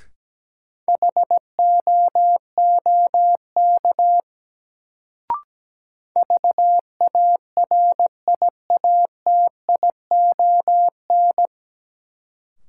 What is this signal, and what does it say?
Morse code, heard as a keyed tone, about 17 wpm, 700 Hz. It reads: HOOK VARIATION
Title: Sentences from Top 1600 Words - New Word Drill Alt - 17wpm